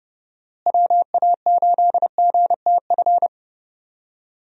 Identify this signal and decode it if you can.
Morse code, heard as a keyed tone, about 30 wpm, 700 Hz. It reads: WA8GTF
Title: USA Call Signs - 6 Characters - Repeated 3x - 30wpm